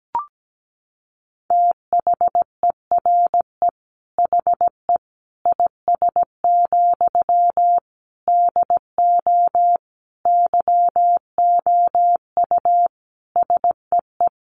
Morse code, heard as a keyed tone, about 17 wpm, 700 Hz. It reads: THERE HE IS, DO YOU SEE
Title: Sentences from Top 100 Words 17wpm